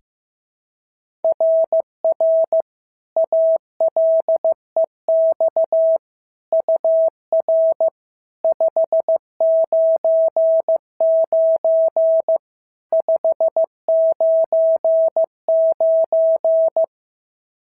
Morse code, heard as a keyed tone, about 15 wpm, 650 Hz. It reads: RR ALEX UR 599 599
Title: SOTA QSO Practice 15wpm